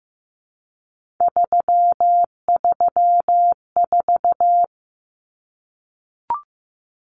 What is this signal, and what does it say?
Morse code, heard as a keyed tone, about 15 wpm, 700 Hz. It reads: 334
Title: Common Cut Numbers - 3-Digit 15wpm